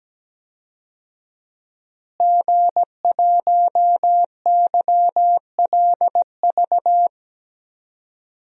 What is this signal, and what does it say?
Morse code, heard as a keyed tone, about 17 wpm, 700 Hz. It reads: G1YLV
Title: DXCC-100 Call Signs - 5 Characters - Repeated 3x - 17wpm